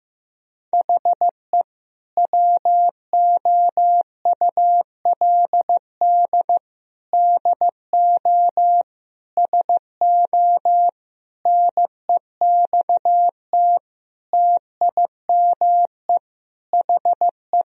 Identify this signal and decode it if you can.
Morse code, heard as a keyed tone, about 15 wpm, 700 Hz. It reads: HE WOULD DO SO NEXT TIME HE
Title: Sentences from Top 1000 Words - Review Part 1 - 15wpm